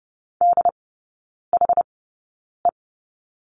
Morse code, heard as a keyed tone, about 30 wpm, spaced as for 9 wpm, 700 Hz. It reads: DHE
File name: 081 Lesson 14e - Number 2 - Three Characters 30wpm